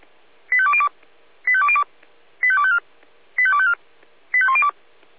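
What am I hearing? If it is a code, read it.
Five-tone selective call (ZVEI), heard as quick runs of five tones, tones of about 70 ms. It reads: 85202 85202 85254 85254 85102
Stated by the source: ZVEI five-tone call received by SDR.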